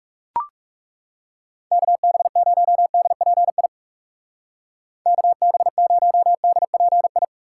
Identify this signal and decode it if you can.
Morse code, heard as a keyed tone, about 42 wpm, 700 Hz. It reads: KB0DPI KB0DPI